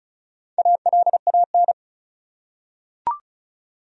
Morse code, heard as a keyed tone, about 36 wpm, 700 Hz. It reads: ALAN
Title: USA Common Names - 35wpm